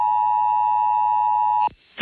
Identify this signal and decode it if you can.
Emergency Alert System (SAME) tones (the 853 and 960 Hz attention tone). It is heard from 0.0 to 1.7 s.